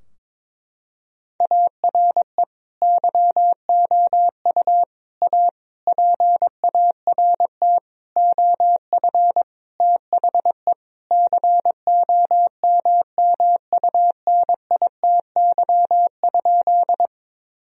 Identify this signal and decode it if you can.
Morse code, heard as a keyed tone, about 22 wpm, 700 Hz. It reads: ARE YOU A PART OF THE COMMUNITY?
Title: Sentences from Top 1000 Words - Review Part 4 - 22wpm